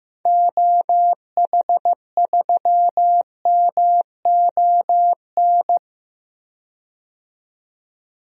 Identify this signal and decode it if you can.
Morse code, heard as a keyed tone, about 15 wpm, 700 Hz. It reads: OH3MON
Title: DXCC-100 Call Signs - 6 Characters - 15wpm